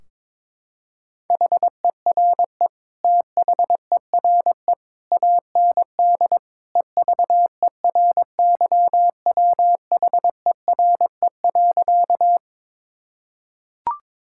Morse code, heard as a keyed tone, about 22 wpm, 700 Hz. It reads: HERE THERE AND EVERYWHERE.
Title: Sentences from Top 1600 Words 22wpm